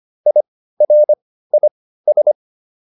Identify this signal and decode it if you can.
Morse code, heard as a keyed tone, about 25 wpm, spaced as for 15 wpm, 600 Hz. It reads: IRIS